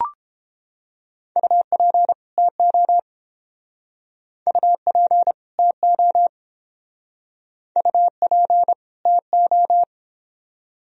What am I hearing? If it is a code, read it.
Morse code, heard as a keyed tone, about 27 wpm, 700 Hz. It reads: UP TO UP TO UP TO
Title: Speed-Racing - Sets of 2 Words 22wpm